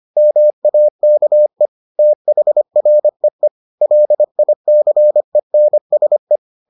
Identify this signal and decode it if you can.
Morse code, heard as a keyed tone, about 25 wpm, 600 Hz. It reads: MAKE THREE LICENSE